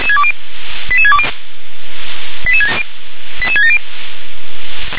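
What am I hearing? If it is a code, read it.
Five-tone selective call (ZVEI), heard as quick runs of five tones, tones of about 70 ms. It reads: 88620 88670
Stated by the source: ZVEI five-tone call received by SDR.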